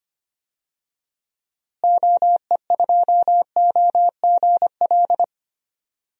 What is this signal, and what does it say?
Morse code, heard as a keyed tone, about 25 wpm, 700 Hz. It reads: OE2OGL
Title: DXCC-100 Call Signs - 6 Characters - Repeated 3x - 25wpm